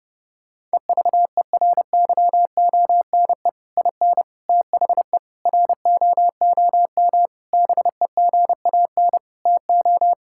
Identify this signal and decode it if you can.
Morse code, heard as a keyed tone, about 30 wpm, 700 Hz. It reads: EVERYONE IN THE ROOM BEGAN TO
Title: Sentences from Top 1000 Words - Review Part 5 - 30wpm